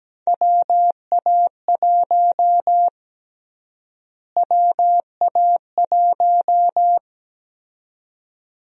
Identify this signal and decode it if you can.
Morse code, heard as a keyed tone, about 17 wpm, 700 Hz. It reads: WA1 WA1
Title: USA Call Sign Prefixes - Repeated 2x - 17wpm